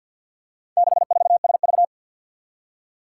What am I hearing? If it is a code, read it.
Morse code, heard as a keyed tone, about 48 wpm, 700 Hz. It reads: 64SV
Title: Four Challenging Letter-Number Combinations 50wpm